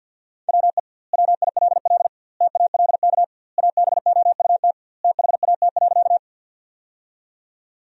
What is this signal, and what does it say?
Morse code, heard as a keyed tone, about 50 wpm, 700 Hz. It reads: WE WILL TALK ABOUT THAT.
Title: 2x Word-Spacing - Sentences from Top 400 Words - 50wpm